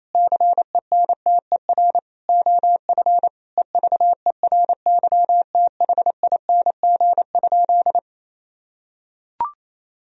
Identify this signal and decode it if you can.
Morse code, heard as a keyed tone, about 28 wpm, 700 Hz. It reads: CENTER OF EVERYTHING?
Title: Sentences from Top 1000 Words - Review Part 2 - 28wpm